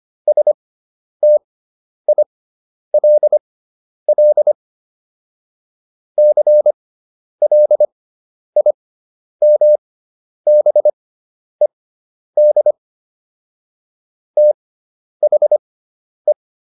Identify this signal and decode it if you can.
Morse code, heard as a keyed tone, about 25 wpm, spaced as for 10 wpm, 600 Hz. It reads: STILL CLIMBED THE